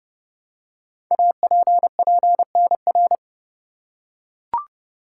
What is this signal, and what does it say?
Morse code, heard as a keyed tone, about 30 wpm, 700 Hz. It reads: APPNR
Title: Code Groups - 5 Letter:Numbers - 30wpm